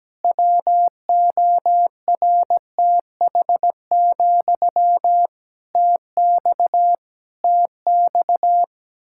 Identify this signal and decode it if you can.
Morse code, heard as a keyed tone, about 17 wpm, 700 Hz. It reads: WORTH, TX TX
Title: Practice QSO - Exchange 1 - Beginning - 17wpm